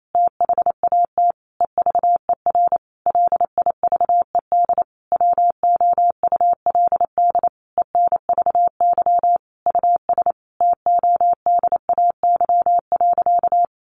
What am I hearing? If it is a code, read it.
Morse code, heard as a keyed tone, about 28 wpm, 700 Hz. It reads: THAT EVER LIVED WOULD ENVY US TODAY.
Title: A Sound Of Thunder 28wpm